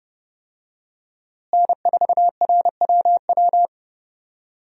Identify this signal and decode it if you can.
Morse code, heard as a keyed tone, about 30 wpm, 700 Hz. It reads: N4RWW
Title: USA Call Signs - 5 Characters - Repeated 2x - 30wpm